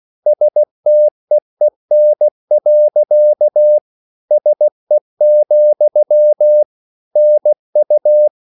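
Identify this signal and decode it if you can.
Morse code, heard as a keyed tone, about 16 wpm, 600 Hz. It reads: STEEN. SE, NU
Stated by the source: Slow (QRS) Morse code at 16 wpm, from CW med Gnister.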